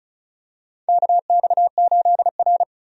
Morse code, heard as a keyed tone, about 36 wpm, 700 Hz. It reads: KX8R
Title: USA Call Signs - 4 Characters - 35wpm